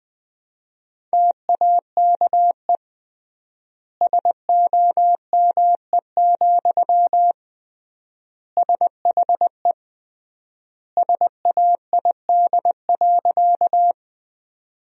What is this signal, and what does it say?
Morse code, heard as a keyed tone, about 20 wpm, 700 Hz. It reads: TAKE SOME, SHE SAID.
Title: 3x Word-Spacing - Sentences from Top 200 Words - 20wpm